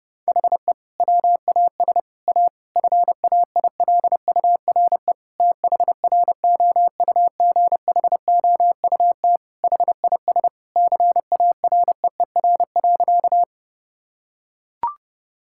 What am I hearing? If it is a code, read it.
Morse code, heard as a keyed tone, about 32 wpm, 700 Hz. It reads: HE WAS A FAILURE THROUGHOUT HIS CAREER.
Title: Sentences from Top 1000 Words - Review Part 6 - 30wpm